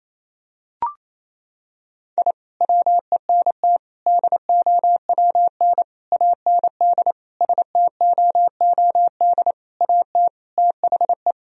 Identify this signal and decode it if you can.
Morse code, heard as a keyed tone, about 28 wpm, 700 Hz. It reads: I WENT DOWN AND STOOD AT THE
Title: Sentences from Top 1000 Words - Review Part 2 - 28wpm